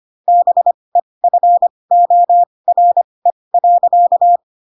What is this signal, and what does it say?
Morse code, heard as a keyed tone, about 25 wpm, spaced as for 20 wpm, 700 Hz. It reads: BEFORE.